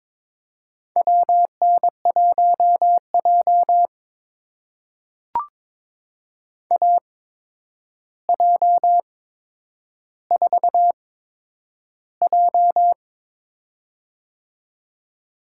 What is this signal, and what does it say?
Morse code, heard as a keyed tone, about 22 wpm, spaced as for 9 wpm, 700 Hz. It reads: WN1J AJ4J
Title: USA Call Signs - 4 Characters - 8x Character Spacing - 22wpm